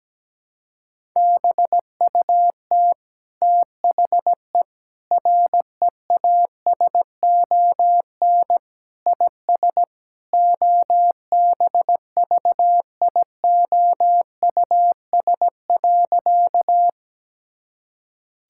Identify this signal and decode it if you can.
Morse code, heard as a keyed tone, about 17 wpm, 700 Hz. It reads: BUT THE REASON IS OBVIOUS.
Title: Sentences from Top 1600 Words 17wpm